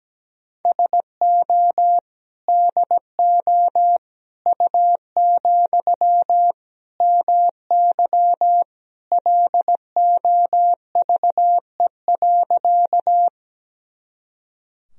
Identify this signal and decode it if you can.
Morse code, heard as a keyed tone, about 17 wpm, 700 Hz. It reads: SO DO U, MY LOVE.
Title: Sentences from Top 400 Words and Abbreviations 17wpm